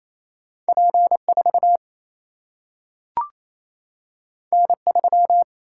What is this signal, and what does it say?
Morse code, heard as a keyed tone, about 28 wpm, 700 Hz. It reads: P 4 N 3
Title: Single-Letter Single-Number 28wpm